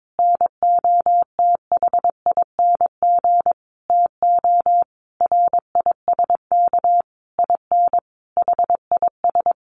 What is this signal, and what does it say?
Morse code, heard as a keyed tone, about 22 wpm, 700 Hz. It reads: NOTHING TO RISK IN HIS
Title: Sentences from Top 600 Words 22wpm